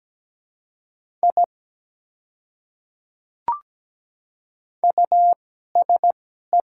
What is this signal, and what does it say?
Morse code, heard as a keyed tone, about 17 wpm, spaced as for 12 wpm, 700 Hz. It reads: I USE
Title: 2x Character Spacing - Top 100 Words - 17wpm